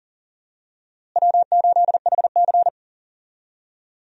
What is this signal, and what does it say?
Morse code, heard as a keyed tone, about 39 wpm, 700 Hz. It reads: W8HC
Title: USA Call Signs - 4 Characters - Repeated 2x - 40wpm